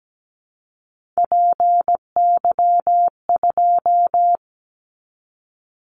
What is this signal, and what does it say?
Morse code, heard as a keyed tone, about 17 wpm, 700 Hz. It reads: PY2
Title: Callsign Prefixes for Top 100 DXCC - 17wpm